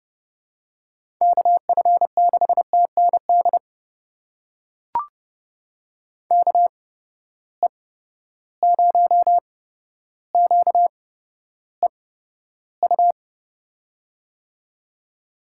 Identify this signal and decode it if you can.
Morse code, heard as a keyed tone, about 32 wpm, spaced as for 13 wpm, 700 Hz. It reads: KF6TND KE0QEU
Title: USA Call Signs - 6 Characters - 8x Character Spacing - 30wpm